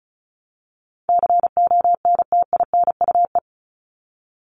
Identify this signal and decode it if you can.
Morse code, heard as a keyed tone, about 34 wpm, 700 Hz. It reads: CONTINUE
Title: Top 751-1000 Words 35wpm